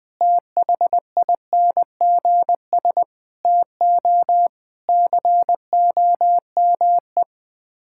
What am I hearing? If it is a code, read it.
Morse code, heard as a keyed tone, about 20 wpm, 700 Hz. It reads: THINGS TO COME
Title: Sets of 6 Words - 20wpm